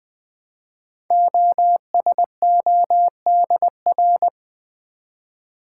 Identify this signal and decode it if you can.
Morse code, heard as a keyed tone, about 20 wpm, 700 Hz. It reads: OSODR
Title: Code Groups - 5 Letters - 20wpm